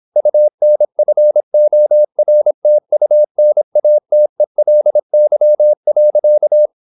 Morse code, heard as a keyed tone, about 26 wpm, 600 Hz. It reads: UNFORTUNATELY.